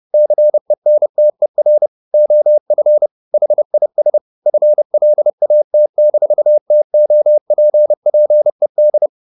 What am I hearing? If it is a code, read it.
Morse code, heard as a keyed tone, about 30 wpm, 600 Hz. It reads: CENTER OF HIS FLAT-TOPPED